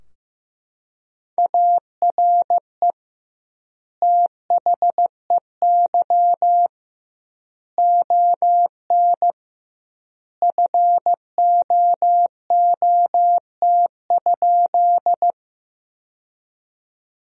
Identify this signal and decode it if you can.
Morse code, heard as a keyed tone, about 15 wpm, 700 Hz. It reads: ARE THEY ON FOOT?
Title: 2x Word-Spacing - Sentences from Top 500 Words - 15wpm